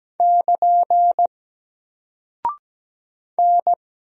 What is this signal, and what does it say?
Morse code, heard as a keyed tone, about 17 wpm, 700 Hz. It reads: <KN> N
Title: Single Character (Letter:Number:Punctuation:Prosign) Alt 17wpm